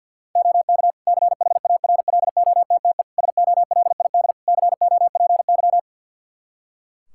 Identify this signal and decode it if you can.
Morse code, heard as a keyed tone, about 49 wpm, 700 Hz. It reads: OK CHARLOTTE SOLID COPY
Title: Practice QSO - Exchange 3 - More Info - 50wpm